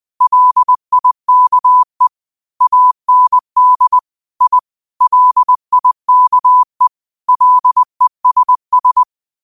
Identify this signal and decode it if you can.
Morse code, heard as a keyed tone, about 20 wpm, 1000 Hz. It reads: LIKE AND I LIKE LESS
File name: qotdfiles / 20wpm